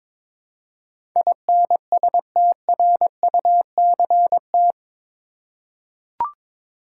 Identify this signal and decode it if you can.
Morse code, heard as a keyed tone, about 22 wpm, 700 Hz. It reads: INSTRUCT
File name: practice / 22wpm / Eight Letter Words - 22wpm